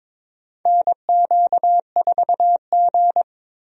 Morse code, heard as a keyed tone, about 22 wpm, 700 Hz. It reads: NQ4G
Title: USA Call Signs - 4 Characters - 22wpm